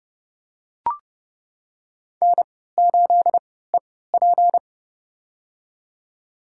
Morse code, heard as a keyed tone, about 30 wpm, spaced as for 17 wpm, 700 Hz. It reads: N8EP
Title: USA Call Signs - 4 Characters - 3x Character Spacing - 30wpm